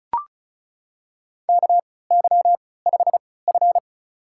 Morse code, heard as a keyed tone, about 36 wpm, spaced as for 20 wpm, 700 Hz. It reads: KY5F
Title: USA Call Signs - 4 Characters - 3x Character Spacing - 35wpm